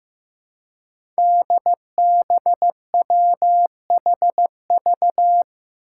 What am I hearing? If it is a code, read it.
Morse code, heard as a keyed tone, about 15 wpm, 700 Hz. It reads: DBWHV
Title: Five Challenging Letter-Number Combinations 15wpm